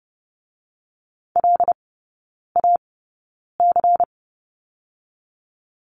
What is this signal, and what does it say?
Morse code, heard as a keyed tone, about 31 wpm, spaced as for 9 wpm, 700 Hz. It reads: LAC
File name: Lesson 18e - Letter m - Three Characters 30wpm